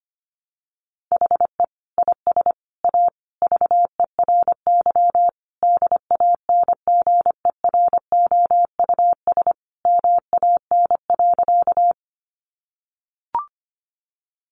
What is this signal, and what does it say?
Morse code, heard as a keyed tone, about 25 wpm, 700 Hz. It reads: HE IS A VERY DANGEROUS MAN.
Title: Sentences from Top 1200 Words 25wpm